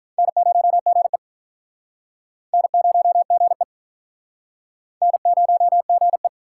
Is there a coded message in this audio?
Yes: Morse code from 0.2 to 6.3 s, heard as a keyed tone, about 47 wpm, 700 Hz. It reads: N0GE N0GE N0GE